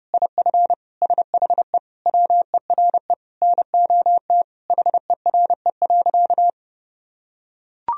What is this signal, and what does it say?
Morse code, heard as a keyed tone, about 30 wpm, 700 Hz. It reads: IF SHE WERE NOT HERE.
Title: Sentences from Top 1000 Words - Review Part 1 - 30wpm